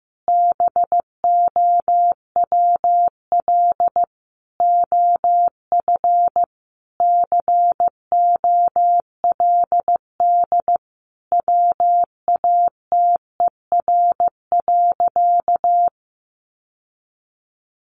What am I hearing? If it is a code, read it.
Morse code, heard as a keyed tone, about 15 wpm, 700 Hz. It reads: BOWL OF COLD WATER.
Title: Sentences from Top 1200 Words 15wpm